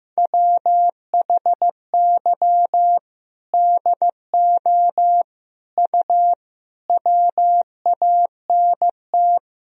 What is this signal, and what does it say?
Morse code, heard as a keyed tone, about 15 wpm, 700 Hz. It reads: WHY DO U WANT
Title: Sentences from Top 200 Words and Abbreviations 15wpm